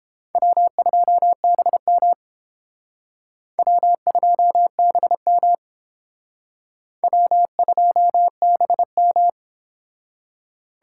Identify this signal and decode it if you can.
Morse code, heard as a keyed tone, about 30 wpm, 700 Hz. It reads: W2BM W2BM W2BM